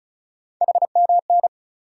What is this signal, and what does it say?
Morse code, heard as a keyed tone, about 34 wpm, 700 Hz. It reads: HMN
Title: USA Call Sign Postfixes - 3 Letters 35wpm